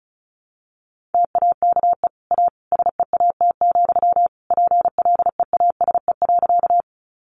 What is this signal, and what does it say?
Morse code, heard as a keyed tone, about 34 wpm, 700 Hz. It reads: TAKE A SEAT, PLEASE.
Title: Sentences from Top 1500 Words 35wpm